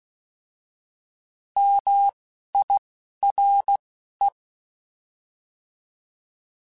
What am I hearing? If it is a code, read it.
Morse code, heard as a keyed tone, about 16 wpm, 800 Hz. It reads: M I R E